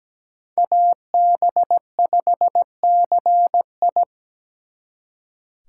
Morse code, heard as a keyed tone, about 20 wpm, 700 Hz. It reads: AB5CI
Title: Speed Racing - USA 2x2 Call Signs 17wpm